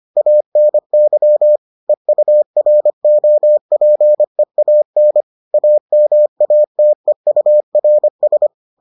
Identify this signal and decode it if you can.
Morse code, heard as a keyed tone, about 25 wpm, 600 Hz. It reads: ANY EUROPEAN AMATEURS